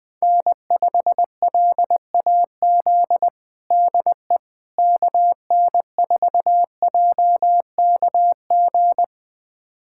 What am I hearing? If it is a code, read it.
Morse code, heard as a keyed tone, about 20 wpm, 700 Hz. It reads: N5LAZ DE KN4JKG
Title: USA Callsign DE Callsign 20wpm